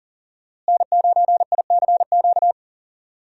Morse code, heard as a keyed tone, about 40 wpm, 700 Hz. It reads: N9ICQ
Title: USA Call Signs - 5 Characters - Repeated 2x - 40wpm